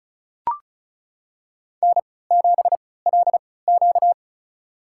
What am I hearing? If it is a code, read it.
Morse code, heard as a keyed tone, about 36 wpm, spaced as for 20 wpm, 700 Hz. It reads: N7LQ
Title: USA Call Signs - 4 Characters - 3x Character Spacing - 35wpm